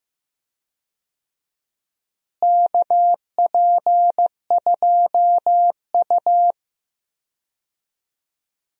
Morse code, heard as a keyed tone, about 15 wpm, 700 Hz. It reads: KP2U
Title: DXCC-100 Call Signs - 4 Characters - Repeated 2x - 15wpm